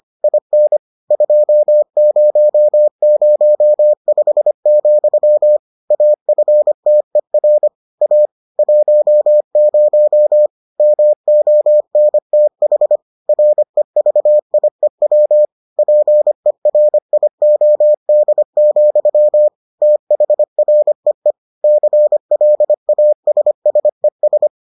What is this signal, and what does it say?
Morse code, heard as a keyed tone, about 25 wpm, 600 Hz. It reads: IN 2005, AFTER A 10 MONTH REVIEW PERIOD, THREE CLASSES